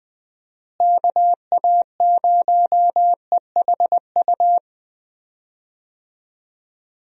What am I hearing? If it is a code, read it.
Morse code, heard as a keyed tone, about 20 wpm, 700 Hz. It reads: KA0EHU
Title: USA Call Signs - 6 Characters - Repeated 3x - 20wpm